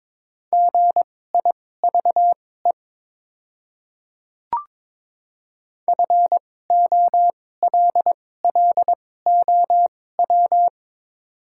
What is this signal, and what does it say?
Morse code, heard as a keyed tone, about 22 wpm, spaced as for 16 wpm, 700 Hz. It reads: GIVE FOLLOW
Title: 2x Character Spacing - Top 200 Words - 22wpm